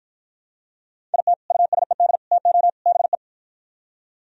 Morse code, heard as a keyed tone, about 52 wpm, 700 Hz. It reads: IT USED TO BE